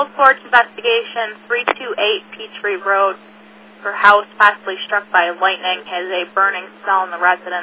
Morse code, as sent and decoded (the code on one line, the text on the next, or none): none